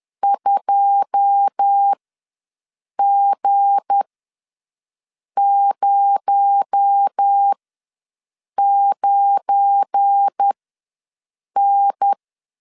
Morse code, heard as a keyed tone, about 10 wpm, spaced as for 6 wpm, 800 Hz. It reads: 2G09N
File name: LEZIONI / 9 2 ws 10wpm 4130